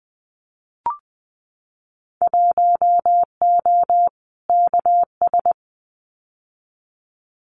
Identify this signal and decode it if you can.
Morse code, heard as a keyed tone, about 19 wpm, 700 Hz. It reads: 1O KS
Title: Winter Field Day Exchanges 20wpm